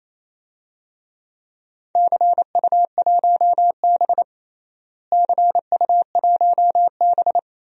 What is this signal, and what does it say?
Morse code, heard as a keyed tone, about 28 wpm, 700 Hz. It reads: CU1B CU1B
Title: DXCC-100 Call Signs - 4 Characters - Repeated 3x - 28wpm